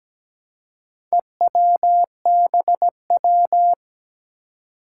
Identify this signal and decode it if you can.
Morse code, heard as a keyed tone, about 17 wpm, 700 Hz. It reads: EWBW
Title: Four Challenging Letter-Number Combinations 17wpm